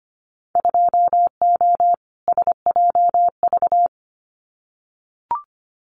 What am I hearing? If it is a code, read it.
Morse code, heard as a keyed tone, about 25 wpm, 700 Hz. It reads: 2O SJV
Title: Winter Field Day Exchanges Encore 25wpm